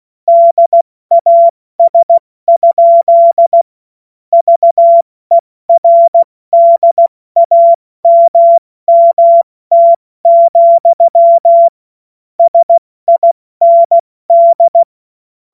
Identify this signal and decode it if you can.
Morse code, heard as a keyed tone, about 16 wpm, 680 Hz. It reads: DAS? VERDAMMT, SIND
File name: Morsekurs / Witze / Witz 27. Januar 2026 70 F80